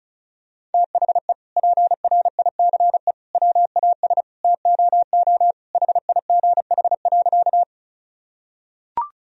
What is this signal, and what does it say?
Morse code, heard as a keyed tone, about 36 wpm, 700 Hz. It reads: THE PRICE WAS TOO HIGH.
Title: Sentences from Top 1000 Words - Review Part 4 - 35wpm